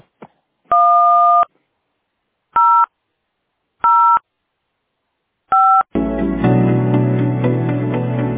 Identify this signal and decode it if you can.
DTMF, heard as telephone touch tones, tones of about 310 ms. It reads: 0 0 5